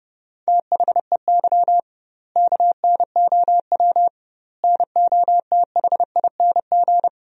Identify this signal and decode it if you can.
Morse code, heard as a keyed tone, about 30 wpm, 700 Hz. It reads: THEY KNOW NOTHING
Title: 2x Word-Spacing - Sentences from Top 500 Words - 30wpm